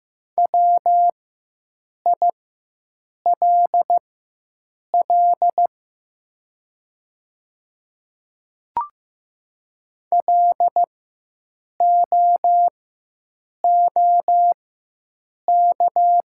Morse code, heard as a keyed tone, about 15 wpm, spaced as for 7 wpm, 700 Hz. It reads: WILL LOOK
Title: 4x Character Spacing - Top 100 Words - 15wpm